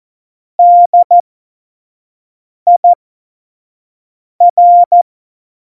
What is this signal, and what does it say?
Morse code, heard as a keyed tone, about 14 wpm, spaced as for 5 wpm, 700 Hz. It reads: DIR
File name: Morsekurs / Witze / Witz 21. Mai 2019, 25 F70